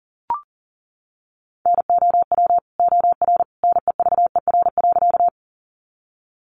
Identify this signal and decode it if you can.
Morse code, heard as a keyed tone, about 40 wpm, 700 Hz. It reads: NOW OR NEVER.
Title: Sentences from Top 300 Words 40wpm